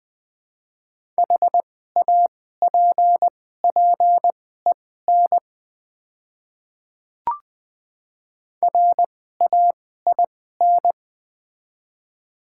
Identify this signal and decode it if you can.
Morse code, heard as a keyed tone, about 20 wpm, spaced as for 14 wpm, 700 Hz. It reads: HAPPEN RAIN
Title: 2x Character Spacing - Top 500 Words - 20wpm